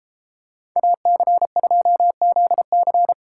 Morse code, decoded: AC2ZC

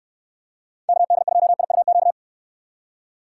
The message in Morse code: KD2IUQ

